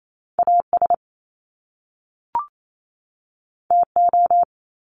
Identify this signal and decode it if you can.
Morse code, heard as a keyed tone, about 28 wpm, 700 Hz. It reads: AS TO
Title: Top Two Letter Combinations in Words 28wpm